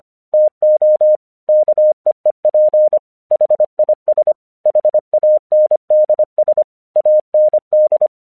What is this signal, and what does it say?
Morse code, heard as a keyed tone, about 25 wpm, 610 Hz. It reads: TO KEEP HIS HANDS AND
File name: 1 English 25wpm / AQC26